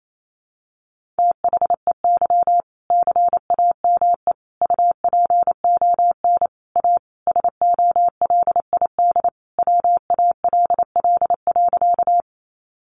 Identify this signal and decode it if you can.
Morse code, heard as a keyed tone, about 28 wpm, 700 Hz. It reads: THEY CAME UPON A SOLID WALL.
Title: Sentences from Top 1500 Words 28wpm